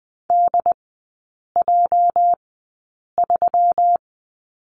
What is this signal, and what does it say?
Morse code, heard as a keyed tone, about 20 wpm, spaced as for 8 wpm, 700 Hz. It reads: DJ3
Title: 205 Lesson 33e - Letter j - Three Characters 20wpm